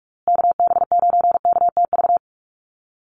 Morse code, heard as a keyed tone, about 44 wpm, 700 Hz. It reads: KB9KTV
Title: USA Call Signs Repeated 2x 45wpm